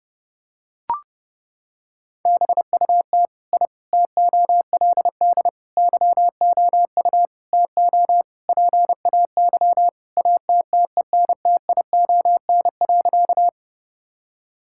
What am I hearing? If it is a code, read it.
Morse code, heard as a keyed tone, about 30 wpm, 700 Hz. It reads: BUT I TOLD YOU TO PAY ATTENTION.
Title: Sentences from Top 1000 Words - Review Part 6 - 30wpm